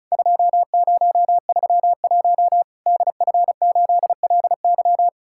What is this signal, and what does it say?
Morse code, heard as a keyed tone, about 34 wpm, 700 Hz. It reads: 2031 DF8LY
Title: Worked All Europe DX Contest - QTC Exchange - 35wpm